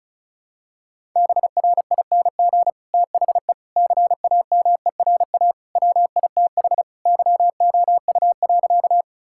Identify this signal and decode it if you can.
Morse code, heard as a keyed tone, about 34 wpm, 700 Hz. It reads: BRING THE CAMERA WITH YOU.
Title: Sentences from Top 1000 Words - Review Part 5 - 35wpm